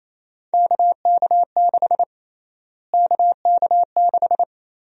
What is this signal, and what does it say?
Morse code, heard as a keyed tone, about 28 wpm, 700 Hz. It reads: KK6 KK6
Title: USA Call Sign Prefixes - Repeated 2x - 28wpm